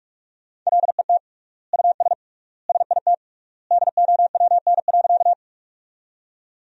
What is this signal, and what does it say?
Morse code, heard as a keyed tone, about 44 wpm, 700 Hz. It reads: LET US SIT DOWN.
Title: 3x Word-Spacing - Sentences from Top 500 Words - 45wpm